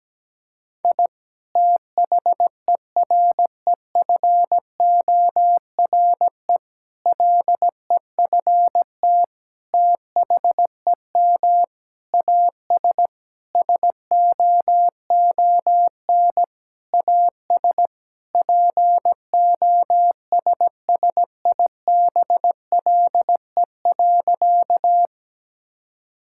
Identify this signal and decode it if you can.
Morse code, heard as a keyed tone, about 17 wpm, 700 Hz. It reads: I THEREFORE LEFT THEM AS SOON AS POSSIBLE.
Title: Sentences from Top 1000 Words - Review Part 5 - 17wpm